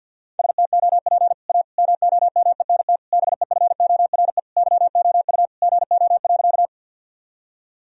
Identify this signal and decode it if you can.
Morse code, heard as a keyed tone, about 51 wpm, 700 Hz. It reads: STOP A MOMENT BEFORE YOU GO.
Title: Sentences from Top 1000 Words - Review Part 1 - 50wpm